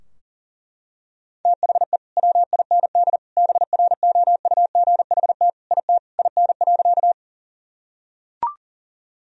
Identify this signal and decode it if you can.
Morse code, heard as a keyed tone, about 37 wpm, 700 Hz. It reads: THE WIND BROUGHT IT IN.
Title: Sentences from Top 1000 Words - Review Part 2 - 40wpm